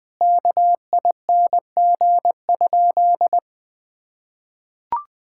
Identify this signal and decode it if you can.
Morse code, heard as a keyed tone, about 19 wpm, 700 Hz. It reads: KING?
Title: Sentences from Top 1000 Words - Review Part 6 - 20wpm